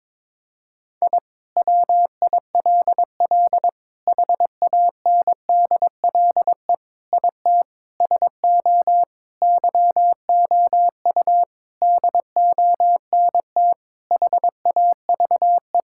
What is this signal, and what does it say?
Morse code, heard as a keyed tone, about 21 wpm, 700 Hz. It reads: I WILL HANDLE IT SO YOU DONT HAVE
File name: Sentences with 7-Word Phrases - 22wpm